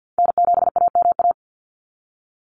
Morse code, heard as a keyed tone, about 49 wpm, 700 Hz. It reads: N7AMA
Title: NA SOTA Chaser Callsigns 50wpm